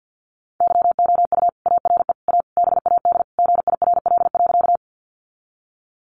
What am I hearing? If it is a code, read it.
Morse code, heard as a keyed tone, about 50 wpm, 700 Hz. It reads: YOU ARE A BAD GIRL.
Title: Sentences from Top 700 Words 50wpm